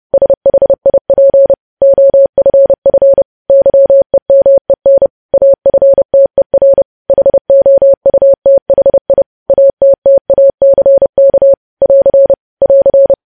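Morse code, heard as a keyed tone, about 30 wpm, 550 Hz. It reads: SHIP OFF YEMEN AFTER HOUTHI ATTACK <AR> <AR>
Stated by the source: Morse news headlines sent at 30.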